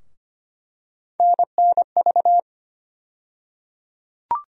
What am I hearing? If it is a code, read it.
Morse code, heard as a keyed tone, about 25 wpm, 700 Hz. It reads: NNV
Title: USA Call Sign Postfixes - 3 Letters 25wpm